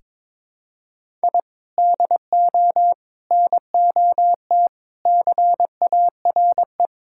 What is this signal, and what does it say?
Morse code, heard as a keyed tone, about 22 wpm, 700 Hz. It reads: I DO NOT CARE
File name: practice / 22wpm / Sentences with 4-Word Phrases - 22wpm